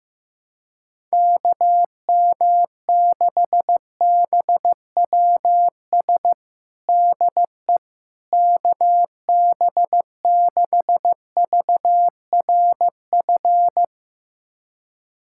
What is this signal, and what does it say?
Morse code, heard as a keyed tone, about 15 wpm, 700 Hz. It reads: KM6BWS DE KB6VRF
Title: USA Callsign DE Callsign - 6 Characters - 15wpm